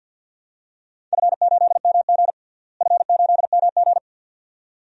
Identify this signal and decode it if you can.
Morse code, heard as a keyed tone, about 48 wpm, 700 Hz. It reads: F8MG F8MG
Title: DXCC-100 Call Signs - 4 Characters - Repeated 2x - 50wpm